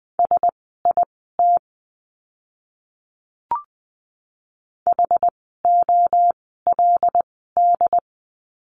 Morse code, heard as a keyed tone, about 20 wpm, spaced as for 15 wpm, 700 Hz. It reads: SIT HOLD
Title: 094 Lesson 16f - Letter c - Words 20wpm